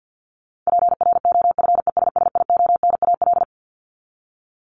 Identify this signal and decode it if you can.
Morse code, heard as a keyed tone, about 50 wpm, 700 Hz. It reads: PROFESSIONAL
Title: Top 501-750 Words 50wpm